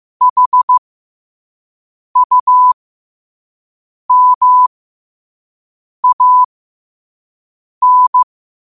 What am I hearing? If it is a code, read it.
Morse code, heard as a keyed tone, about 15 wpm, spaced as for 5 wpm, 1000 Hz. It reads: HUMAN
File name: qotdfiles / 5wpm